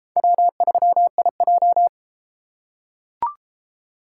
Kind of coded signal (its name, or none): Morse code